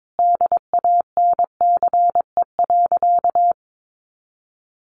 Morse code, heard as a keyed tone, about 22 wpm, 700 Hz. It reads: DANCE.
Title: Sentences from Top 900 Words 22wpm